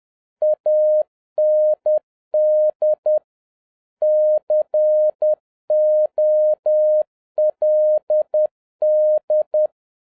Morse code, heard as a keyed tone, about 10 wpm, 610 Hz. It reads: AND COLD